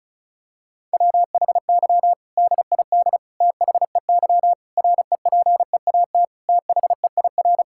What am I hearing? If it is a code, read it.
Morse code, heard as a keyed tone, about 36 wpm, 700 Hz. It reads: WHY DID THEY REPEAT THEIR
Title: Sentences from Top 1600 Words 35wpm